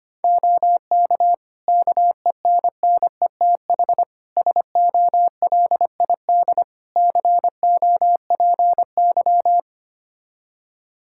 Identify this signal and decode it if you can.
Morse code, heard as a keyed tone, about 25 wpm, 700 Hz. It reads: OK KENNETH SOLID COPY
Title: Practice QSO - Exchange 3 - More Info - 25wpm